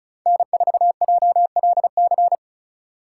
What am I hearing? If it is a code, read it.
Morse code, heard as a keyed tone, about 36 wpm, 700 Hz. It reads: N4JLC
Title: USA Call Signs - 5 Characters - 3x Character Spacing - 35wpm